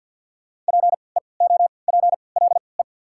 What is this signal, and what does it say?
Morse code, heard as a keyed tone, about 48 wpm, spaced as for 28 wpm, 700 Hz. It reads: PEOPLE
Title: 3x Character Spacing - Top 100 Words - 50wpm